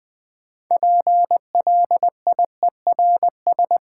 Morse code, heard as a keyed tone, about 20 wpm, 690 Hz. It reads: PLIERS
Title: Difficult Words with Hints - 20wpm